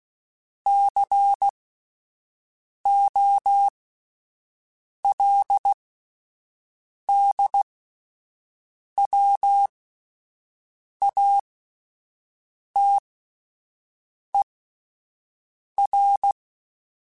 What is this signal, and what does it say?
Morse code, heard as a keyed tone, about 16 wpm, spaced as for 5 wpm, 800 Hz. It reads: COLDWATER